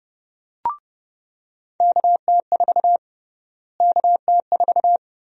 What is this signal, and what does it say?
Morse code, heard as a keyed tone, about 30 wpm, 700 Hz. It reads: KT4 KT4
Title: USA Call Sign Prefixes - Repeated 2x - 30wpm